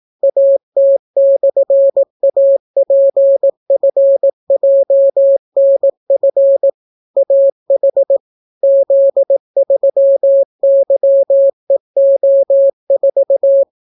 Morse code, heard as a keyed tone, about 18 wpm, 550 Hz. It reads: AT/APFJNF AH Z3YEO4